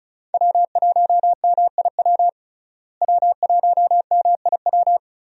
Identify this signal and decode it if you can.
Morse code, heard as a keyed tone, about 36 wpm, 700 Hz. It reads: W1MIW W1MIW
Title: USA Call Signs - 5 Characters - Repeated 3x - 35wpm